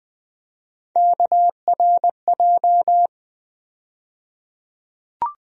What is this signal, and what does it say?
Morse code, heard as a keyed tone, about 20 wpm, 700 Hz. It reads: KRJ
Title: USA Call Sign Postfixes - 3 Letters 20wpm